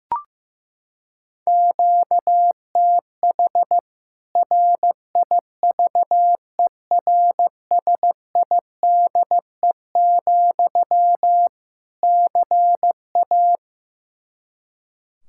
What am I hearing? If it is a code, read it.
Morse code, heard as a keyed tone, about 15 wpm, 700 Hz. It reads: QTH RIVERSIDE, CA
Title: K1USN SST QSO Practice 15wpm